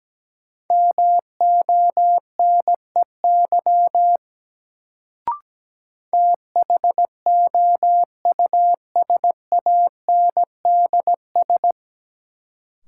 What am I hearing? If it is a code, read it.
Morse code, heard as a keyed tone, about 17 wpm, 700 Hz. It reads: MONEY THOUSANDS
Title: Sets of 4 Words - New Word Drill - 17wpm